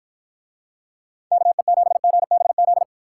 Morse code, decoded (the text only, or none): KE7GBZ